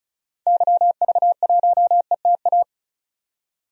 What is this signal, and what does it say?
Morse code, heard as a keyed tone, about 34 wpm, 700 Hz. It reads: YV1ETA
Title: DXCC-100 Call Signs - 6 Characters - 35wpm